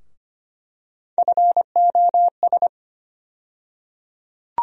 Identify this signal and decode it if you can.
Morse code, heard as a keyed tone, about 25 wpm, 700 Hz. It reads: FOS
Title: USA Call Sign Postfixes - 3 Letters 25wpm